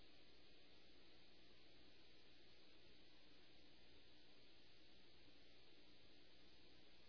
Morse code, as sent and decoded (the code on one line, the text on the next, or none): none